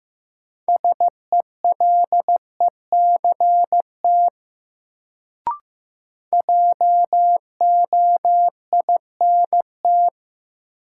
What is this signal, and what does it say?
Morse code, heard as a keyed tone, about 15 wpm, 700 Hz. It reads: SELECT JOINT
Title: Sentences from Top 1300 Words - New Word Drill - 15wpm